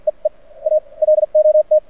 Morse code, heard as a keyed tone, about 48 wpm, spaced as for 42 wpm, 610 Hz. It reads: E E A LOT
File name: archived / CRSnet